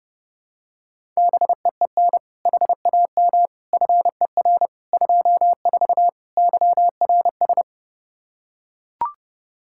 Morse code, heard as a keyed tone, about 30 wpm, 700 Hz. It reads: BEEN HAM FER 24 YRS
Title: Practice QSO - Exchange 3 - More Info - 30wpm